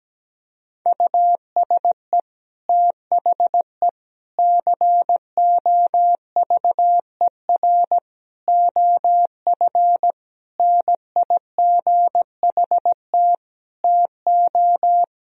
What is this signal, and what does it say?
Morse code, heard as a keyed tone, about 17 wpm, 700 Hz. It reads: USE THE COVER OF NIGHT TO
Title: Sentences from Top 1000 Words - Review Part 1 - 17wpm